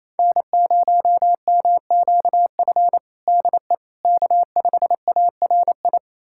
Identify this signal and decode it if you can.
Morse code, heard as a keyed tone, about 28 wpm, 700 Hz. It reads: N0MQF DE K5ARI
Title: USA Callsign DE Callsign 28wpm